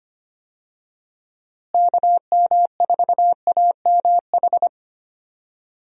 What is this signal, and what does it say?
Morse code, heard as a keyed tone, about 25 wpm, 700 Hz. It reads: KM4AMH
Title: USA Call Signs - 6 Characters - Repeated 3x - 25wpm